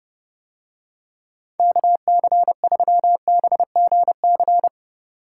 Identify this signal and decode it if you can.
Morse code, heard as a keyed tone, about 30 wpm, 700 Hz. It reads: KC3BGC